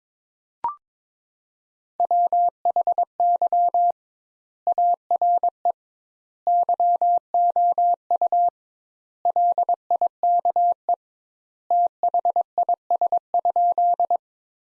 Morse code, heard as a keyed tone, about 22 wpm, 700 Hz. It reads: WHY ARE YOU LIKE THIS?
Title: Sentences Top 200 Words - 22wpm